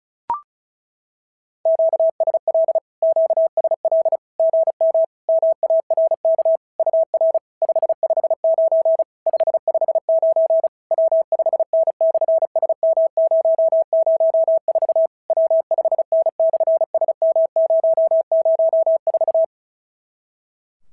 Morse code, heard as a keyed tone, about 36 wpm, 650 Hz. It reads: QSL QSL GM MARK UR 559 559 W5N/SM004 W5N/SM004
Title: SOTA QSO Practice 35wpm